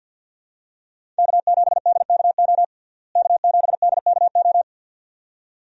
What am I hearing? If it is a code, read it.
Morse code, heard as a keyed tone, about 52 wpm, 700 Hz. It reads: K7DKO K7DKO
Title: USA Call Signs - 5 Characters - Repeated 2x - 50wpm